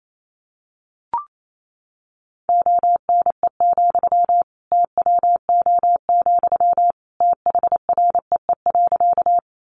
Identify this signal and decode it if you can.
Morse code, heard as a keyed tone, about 28 wpm, 700 Hz. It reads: ONE, TWO, THREE.
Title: Sentences from Top 200 Words 28wpm